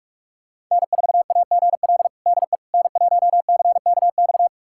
Morse code, heard as a keyed tone, about 46 wpm, 700 Hz. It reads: N4AGL DE N1CKX
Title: USA Callsign DE Callsign - 5 Characters - 45wpm